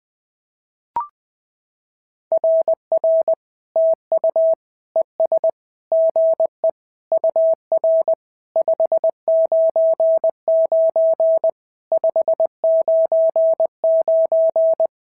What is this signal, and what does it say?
Morse code, heard as a keyed tone, about 20 wpm, 650 Hz. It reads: RR TU ES GE UR 599 599
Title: SOTA QSO Practice 20wpm